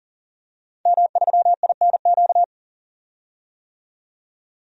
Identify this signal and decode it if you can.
Morse code, heard as a keyed tone, about 40 wpm, 700 Hz. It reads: M3INQ
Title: DXCC-100 Call Signs - 5 Characters - 40wpm